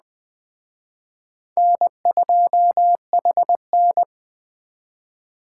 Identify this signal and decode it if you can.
Morse code, heard as a keyed tone, about 20 wpm, 700 Hz. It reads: N2HN
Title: Speed Racing - USA Call Signs 17wpm